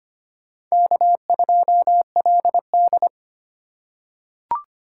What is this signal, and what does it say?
Morse code, heard as a keyed tone, about 25 wpm, 700 Hz. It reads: K2LD
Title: USA Call Signs - 4 Characters - 3x Character Spacing - 25wpm